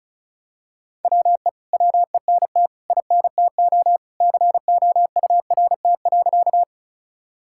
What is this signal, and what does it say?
Morse code, heard as a keyed tone, about 36 wpm, 700 Hz. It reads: WE WENT INTO COURT.
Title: Sentences from Top 1600 Words 35wpm